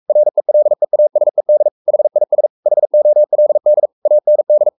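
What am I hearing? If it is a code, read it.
Morse code, heard as a keyed tone, about 44 wpm, 600 Hz. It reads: RELEASED HIS HOLD AND
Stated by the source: Morse code at 43 wpm sending Ken Follett's novel Never in English, from CW med Gnister.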